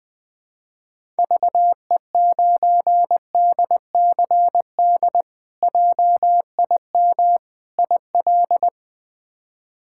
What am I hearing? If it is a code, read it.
Morse code, heard as a keyed tone, about 20 wpm, 700 Hz. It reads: VE9DCD JIM IL
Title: K1USN SST QSO Practice - 20wpm